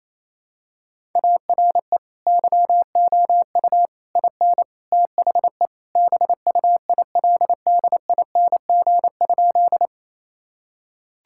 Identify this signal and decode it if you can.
Morse code, heard as a keyed tone, about 27 wpm, 700 Hz. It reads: ARE YOU IN THE BUILDING?
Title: Sentences from Top 1000 Words - Review Part 5 - 28wpm